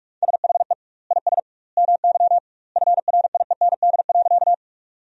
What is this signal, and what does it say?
Morse code, heard as a keyed tone, about 46 wpm, 700 Hz. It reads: SHE IS MY FRIEND.
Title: 2x Word-Spacing - Sentences from Top 300 Words - 45wpm